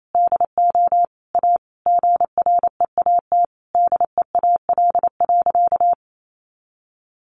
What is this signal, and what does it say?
Morse code, heard as a keyed tone, about 28 wpm, 700 Hz. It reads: DO A GREAT DEAL.
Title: Sentences from Top 1300 Words 28wpm